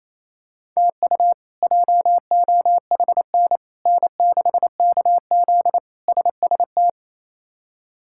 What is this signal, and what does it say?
Morse code, heard as a keyed tone, about 28 wpm, 700 Hz. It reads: TU JOHN N6KZ SST
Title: K1USN SST QSO Practice - 28wpm